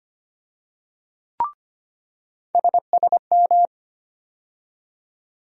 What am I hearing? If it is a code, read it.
Morse code, heard as a keyed tone, about 25 wpm, 700 Hz. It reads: SSM
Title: USA Call Sign Postfixes - 3 Letters 25wpm